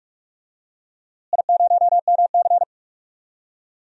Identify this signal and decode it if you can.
Morse code, heard as a keyed tone, about 44 wpm, 700 Hz. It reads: I0MC
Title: DXCC-100 Call Signs - 4 Characters - Repeated 3x - 45wpm